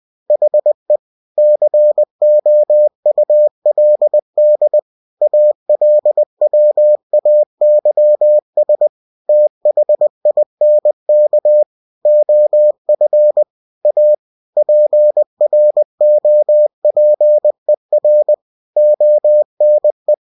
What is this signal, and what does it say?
Morse code, heard as a keyed tone, about 20 wpm, 600 Hz. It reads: HE COULD ALWAYS THINK OF A PROPER ONE
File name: cw-books / winnie-the-pooh / winnie-20wpm-page-0048